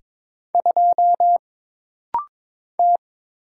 Morse code, heard as a keyed tone, about 22 wpm, 700 Hz. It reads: 2 T
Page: This rapid-fire timing is geared to push your ICR (Instant Character Recognition) skills.